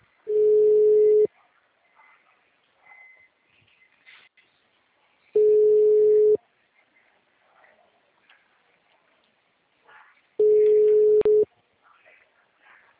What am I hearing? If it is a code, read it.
Ringback tone (Europe, 425 Hz).